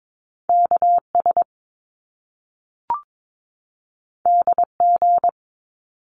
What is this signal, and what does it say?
Morse code, heard as a keyed tone, about 22 wpm, 700 Hz. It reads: KS DG
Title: Top Two Letter Combinations in Words 22wpm